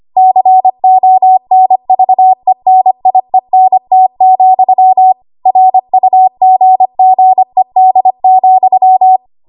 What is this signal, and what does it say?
Morse code, heard as a keyed tone, about 25 wpm, 750 Hz. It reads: CONVENIENT, RUGGED,
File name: Morse / 25WPM